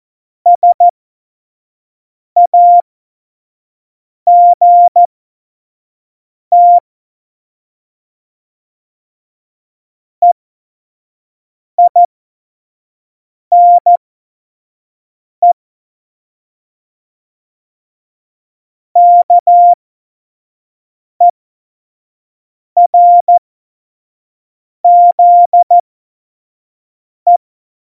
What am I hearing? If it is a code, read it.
Morse code, heard as a keyed tone, about 14 wpm, spaced as for 5 wpm, 700 Hz. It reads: SAGT EINE KERZE
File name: Morsekurs / Witze / Witz 17. Oktober 17, Tempo 25